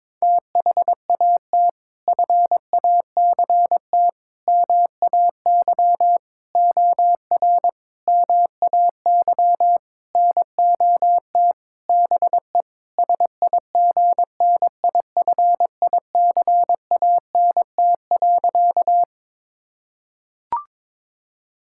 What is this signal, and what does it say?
Morse code, heard as a keyed tone, about 21 wpm, 700 Hz. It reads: THAT FACT MAY OR MAY NOT BE SIGNIFICANT.
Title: Sentences from Top 1000 Words - Review Part 6 - 22wpm